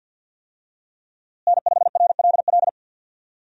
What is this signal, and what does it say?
Morse code, heard as a keyed tone, about 51 wpm, 700 Hz. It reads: N5RLL
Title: USA Call Signs - 5 Characters - Repeated 2x - 50wpm